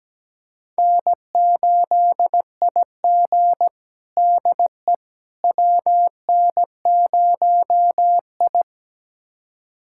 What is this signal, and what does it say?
Morse code, heard as a keyed tone, about 17 wpm, 700 Hz. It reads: N8IG DE WN0I
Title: USA Callsign DE Callsign - 4 Characters - 17wpm